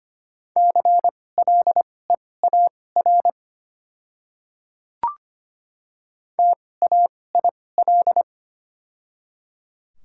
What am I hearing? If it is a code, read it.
Morse code, heard as a keyed tone, about 25 wpm, spaced as for 18 wpm, 700 Hz. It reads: CLEAR TAIL
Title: 2x Character Spacing - Top 500 Words - 25wpm